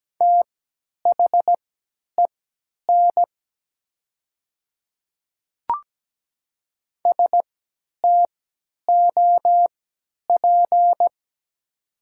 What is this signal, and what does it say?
Morse code, heard as a keyed tone, about 17 wpm, spaced as for 10 wpm, 700 Hz. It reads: THEN STOP
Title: 3x Character Spacing - Top 500 Words - 17wpm